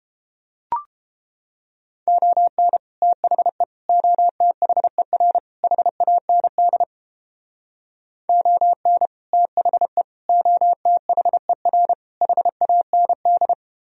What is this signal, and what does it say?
Morse code, heard as a keyed tone, about 32 wpm, 700 Hz. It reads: ON THE OTHER HAND ON THE OTHER HAND